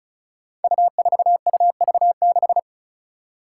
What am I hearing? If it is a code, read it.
Morse code, heard as a keyed tone, about 35 wpm, 700 Hz. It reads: U4UV6
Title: Five Challenging Letter-Number Combinations 35wpm